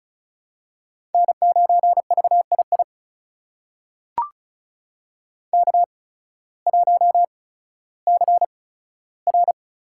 Morse code, heard as a keyed tone, about 36 wpm, spaced as for 16 wpm, 700 Hz. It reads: N9VII K1CR
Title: USA Call Signs - 5 Characters - 8x Character Spacing - 35wpm